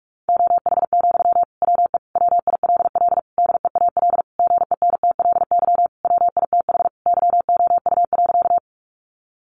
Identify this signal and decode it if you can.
Morse code, heard as a keyed tone, about 44 wpm, 700 Hz. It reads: OH, WE WILL DEAL GENTLY WITH YOU.
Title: Sentences from Top 1500 Words 45wpm